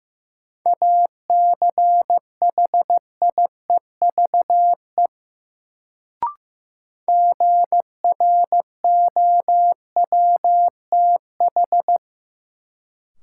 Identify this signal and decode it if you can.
Morse code, heard as a keyed tone, about 15 wpm, 700 Hz. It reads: ACHIEVE GROWTH
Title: Sentences from Top 1000 Words - Review Part 5 - New Word Drill - 15wpm